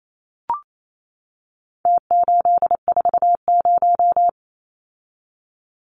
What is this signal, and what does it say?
Morse code, heard as a keyed tone, about 28 wpm, 700 Hz. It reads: T840
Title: Numbers - Contest Style - 4-Digit 28wpm